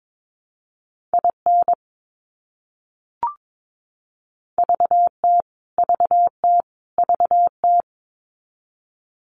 Morse code, heard as a keyed tone, about 22 wpm, 700 Hz. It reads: IN VT VT VT
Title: US States Repeated 3x 22wpm